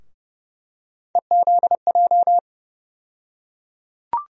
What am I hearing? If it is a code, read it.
Morse code, heard as a keyed tone, about 31 wpm, 700 Hz. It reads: EZJ
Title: USA Call Sign Postfixes - 3 Letters 30wpm